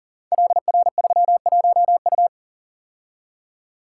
Morse code, heard as a keyed tone, about 42 wpm, 690 Hz. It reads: LR31U
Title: Code Groups - 5 Letter:Numbers - 40wpm